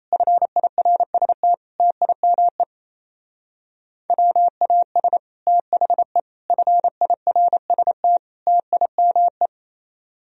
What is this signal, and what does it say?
Morse code, heard as a keyed tone, about 33 wpm, 700 Hz. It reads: FIRST TIME WAS THE FIRST TIME
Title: Speed-Racing - Sets of 4 Words 28wpm